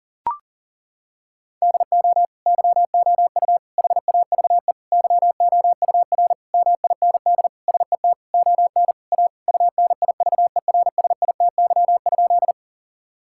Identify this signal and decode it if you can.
Morse code, heard as a keyed tone, about 40 wpm, 700 Hz. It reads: DO YOU HAVE YOUR MIND SET ON A UNIVERSITY?
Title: Sentences from Top 1000 Words - Review Part 5 - 40wpm